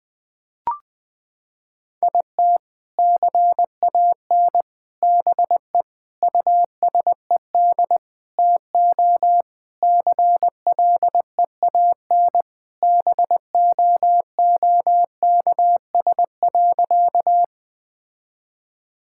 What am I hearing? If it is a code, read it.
Morse code, heard as a keyed tone, about 20 wpm, 700 Hz. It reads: IT CAN BE USED TO CLEAN BOOKS.
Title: Sentences with 4-Word Phrases - 20wpm